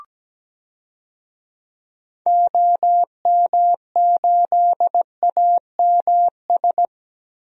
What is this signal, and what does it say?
Morse code, heard as a keyed tone, about 17 wpm, 700 Hz. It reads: OM8AMS
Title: DXCC-100 Call Signs - 6 Characters - Repeated 3x - 17wpm